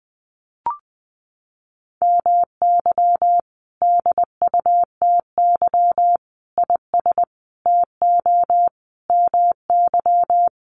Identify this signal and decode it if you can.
Morse code, heard as a keyed tone, about 20 wpm, 700 Hz. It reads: MY DUTY IS TO MY
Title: Sentences from Top 1500 Words 20wpm